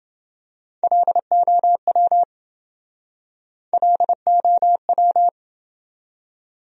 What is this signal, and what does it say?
Morse code, heard as a keyed tone, about 29 wpm, 700 Hz. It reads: LOW LOW